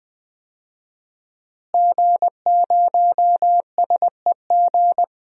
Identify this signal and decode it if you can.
Morse code, heard as a keyed tone, about 20 wpm, 700 Hz. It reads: G0SEG
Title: DXCC-100 Call Signs - 5 Characters - Repeated 3x - 20wpm